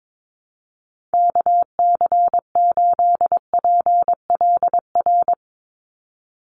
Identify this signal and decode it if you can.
Morse code, heard as a keyed tone, about 22 wpm, 700 Hz. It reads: KC8PLR
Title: USA Call Signs 22wpm